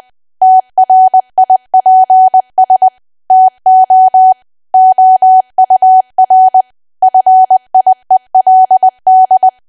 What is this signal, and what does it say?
Morse code, heard as a keyed tone, about 20 wpm, 750 Hz. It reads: TRIPS TO OUR FIELD
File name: Archive / 20 WPM / 20WPM